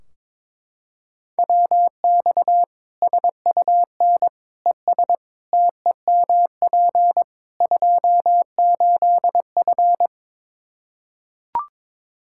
Morse code, heard as a keyed tone, about 22 wpm, 700 Hz. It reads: WX SUN ES TEMP 28F
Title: Practice QSO - Exchange 2 - Info - 22wpm